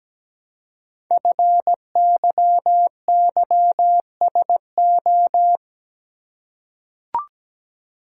Morse code, heard as a keyed tone, about 17 wpm, 700 Hz. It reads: FYYSO
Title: Code Groups - 5 Letters - 17wpm